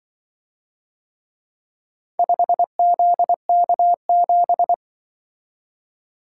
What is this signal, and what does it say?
Morse code, heard as a keyed tone, about 24 wpm, 700 Hz. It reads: 5ZK7
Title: Speed-Racing - Four Letter-Number Combinations - 20wpm